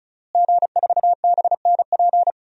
Morse code, heard as a keyed tone, about 36 wpm, 700 Hz. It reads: G4BNP
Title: DXCC-100 Call Signs - 5 Characters - Repeated 3x - 35wpm